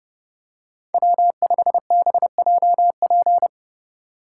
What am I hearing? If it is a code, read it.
Morse code, heard as a keyed tone, about 29 wpm, 700 Hz. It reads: W5BJP